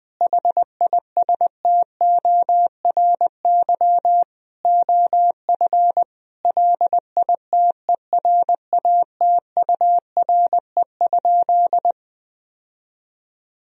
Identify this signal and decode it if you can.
Morse code, heard as a keyed tone, about 20 wpm, 700 Hz. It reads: HISTORY OF LITERATURE?
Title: Sentences from Top 1000 Words - Review Part 3 - 20wpm